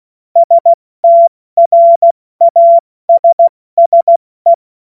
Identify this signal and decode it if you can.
Morse code, heard as a keyed tone, about 16 wpm, 680 Hz. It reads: STRASSE
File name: Morsekurs / Witze / Witz 02 April 2024 70 F80